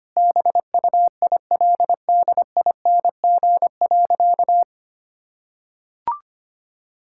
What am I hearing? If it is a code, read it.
Morse code, heard as a keyed tone, about 25 wpm, 700 Hz. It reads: BUILDING.
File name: practice / 25wpm / Sentences with 3-Word Phrases - 25wpm